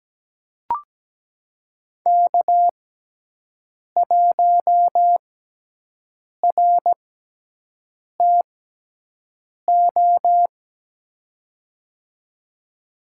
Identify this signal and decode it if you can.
Morse code, heard as a keyed tone, about 17 wpm, spaced as for 6 wpm, 700 Hz. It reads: K1RTO
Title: USA Call Signs - 5 Characters - 6x Character Spacing - 17wpm